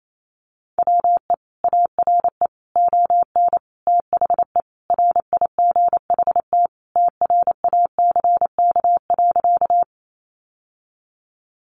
Morse code, heard as a keyed tone, about 28 wpm, 700 Hz. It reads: WE ARE ON THE RIGHT TRACK.
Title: Sentences from Top 1100 Words 28wpm